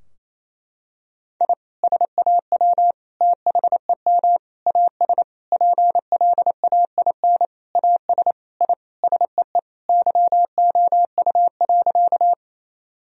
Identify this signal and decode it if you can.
Morse code, heard as a keyed tone, about 28 wpm, 700 Hz. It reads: I SAW THEM AS PLAIN AS I SEE YOU.
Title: Sentences from Top 1000 Words - Review Part 2 - 28wpm